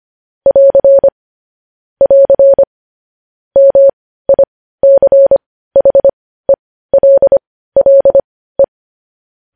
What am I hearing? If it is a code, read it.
Morse code, heard as a keyed tone, about 25 wpm, spaced as for 15 wpm, 550 Hz. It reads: <AR> <AR> MICHELLE
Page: news-headlines-in-morse-code-at-15-wpm